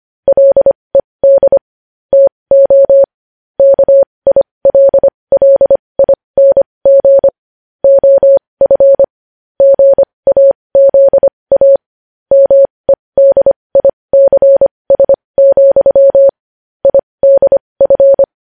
Morse code, heard as a keyed tone, about 25 wpm, spaced as for 20 wpm, 550 Hz. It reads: LED TO KILLING OF GAZA MEDICS, IDF